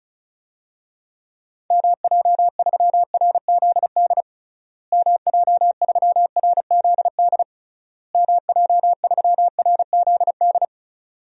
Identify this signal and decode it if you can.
Morse code, heard as a keyed tone, about 36 wpm, 700 Hz. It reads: MJ3RZD MJ3RZD MJ3RZD
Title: DXCC-100 Call Signs - 6 Characters - Repeated 3x - 35wpm